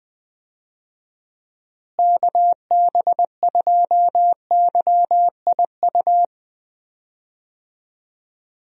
Morse code, heard as a keyed tone, about 21 wpm, 700 Hz. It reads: KB2YIU
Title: USA Call Signs - 6 Characters - Repeated 3x - 20wpm